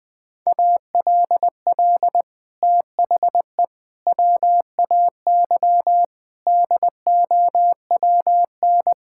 Morse code, heard as a keyed tone, about 20 wpm, 700 Hz. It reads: ALL THE WAY DOWN
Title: Sets of 6 Words - 20wpm